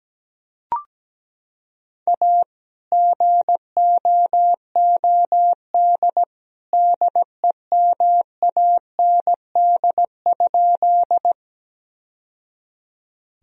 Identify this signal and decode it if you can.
Morse code, heard as a keyed tone, about 17 wpm, 700 Hz. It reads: A GOOD DEMAND?
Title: Sentences from Top 1000 Words - Review Part 4 - 17wpm